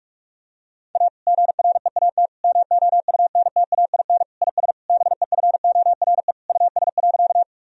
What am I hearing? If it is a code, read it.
Morse code, heard as a keyed tone, about 43 wpm, 700 Hz. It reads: A GREAT MOUNTAIN IS BEFORE US.
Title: Sentences from Top 1000 Words - Review Part 1 - 45wpm